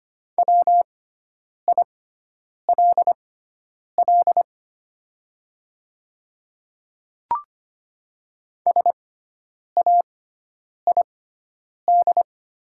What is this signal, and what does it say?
Morse code, heard as a keyed tone, about 25 wpm, spaced as for 9 wpm, 700 Hz. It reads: WILL SAID
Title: 6x Character Spacing - Top 200 Words - 25wpm